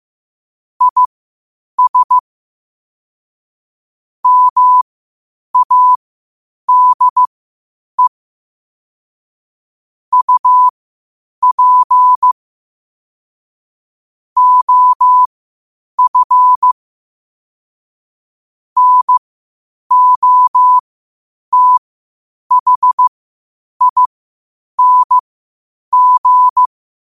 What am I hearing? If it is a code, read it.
Morse code, heard as a keyed tone, about 15 wpm, spaced as for 8 wpm, 1000 Hz. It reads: IS MADE UP OF NOTHING
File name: qotdfiles / 8wpm